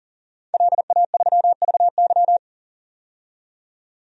Morse code, heard as a keyed tone, about 40 wpm, 700 Hz. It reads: LA3VY
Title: DXCC-100 Call Signs - 5 Characters - 40wpm